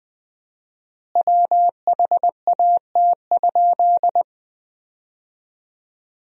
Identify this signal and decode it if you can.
Morse code, heard as a keyed tone, about 20 wpm, 700 Hz. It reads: WHAT?
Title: 4x Word-Spacing - Sentences from Top 300 Words - 20wpm